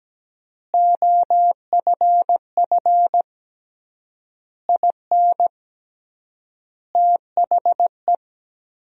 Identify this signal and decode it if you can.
Morse code, heard as a keyed tone, about 17 wpm, 700 Hz. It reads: OFF IN THE
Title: 3x Word-Spacing - Sentences from Top 200 Words - 17wpm